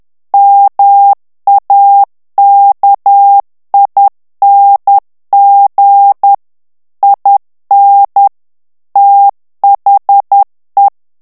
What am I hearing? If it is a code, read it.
Morse code, heard as a keyed tone, about 10 wpm, 800 Hz. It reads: MAKING IN THE
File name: ANGLAIS texte clair 3 vitesse 10 mots minute